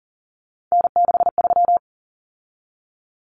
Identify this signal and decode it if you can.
Morse code, heard as a keyed tone, about 40 wpm, 700 Hz. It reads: N63
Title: Common Cut Numbers - 3-Digit 40wpm